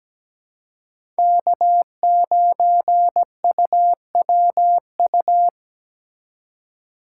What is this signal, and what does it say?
Morse code, heard as a keyed tone, about 17 wpm, 700 Hz. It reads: K9UWU
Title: USA Call Signs - 5 Characters - 2x Character Spacing - 17wpm